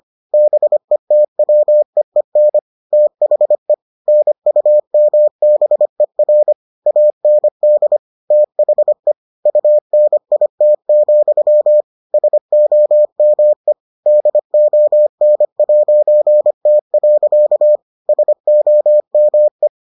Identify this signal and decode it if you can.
Morse code, heard as a keyed tone, about 25 wpm, 600 Hz. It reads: BETWEEN THE NUMBER AND THE UNIT, SOME DON'T. SOME